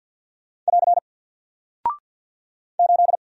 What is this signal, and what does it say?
Morse code, heard as a keyed tone, about 48 wpm, 700 Hz. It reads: <AR> 8
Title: Single Character (Letter:Number:Punctuation:Prosign) Alt 50wpm